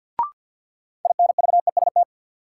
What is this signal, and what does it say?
Morse code, heard as a keyed tone, about 52 wpm, 700 Hz. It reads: INVEST
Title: Sentences from Top 1100 Words - New Word Drill - 50wpm